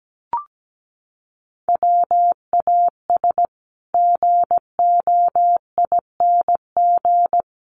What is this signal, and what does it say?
Morse code, heard as a keyed tone, about 17 wpm, 700 Hz. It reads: WAS GOING
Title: Sets of 5 Words Encore2 17wpm